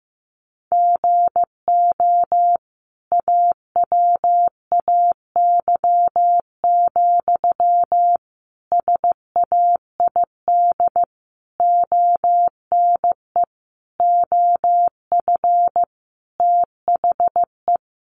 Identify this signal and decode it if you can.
Morse code, heard as a keyed tone, about 15 wpm, 700 Hz. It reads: GO AWAY, SAID ONE OF THE
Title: Sentences from Top 1500 Words 15wpm